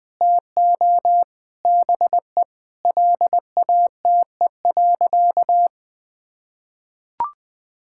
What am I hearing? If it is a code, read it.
Morse code, heard as a keyed tone, about 20 wpm, 700 Hz. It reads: TO BE LATE.
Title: Sentences with 6-Word Phrases - 20wpm